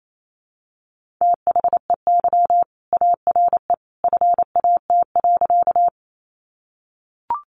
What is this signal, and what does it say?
Morse code, heard as a keyed tone, about 28 wpm, 700 Hz. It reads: THEY ARE FAT.
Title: Sentences from Top 600 Words 28wpm